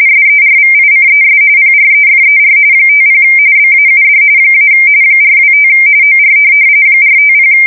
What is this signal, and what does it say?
RTTY, about 45.45 baud. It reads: SERVICES PREFER 50 BAUD - 425 OR 450
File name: fsk170 / teletype-RTTY